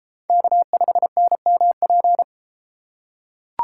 Morse code, heard as a keyed tone, about 32 wpm, 700 Hz. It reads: K5NMP